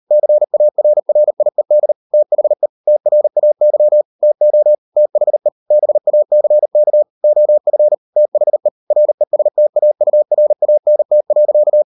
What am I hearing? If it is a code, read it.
Morse code, heard as a keyed tone, about 38 wpm, 600 Hz. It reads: CARRIED THE TRAY TO THE BACK OF THE RESTAURANT.